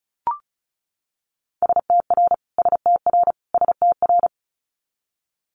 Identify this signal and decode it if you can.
Morse code, heard as a keyed tone, about 36 wpm, 700 Hz. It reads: STR STR STR
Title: Multi-letter Phonemes Repeated 3x 35wpm